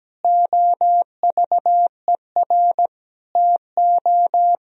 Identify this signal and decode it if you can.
Morse code, heard as a keyed tone, about 17 wpm, 700 Hz. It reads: OVER TO
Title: Sentences from Top 1000 Words - Review Part 2 - 17wpm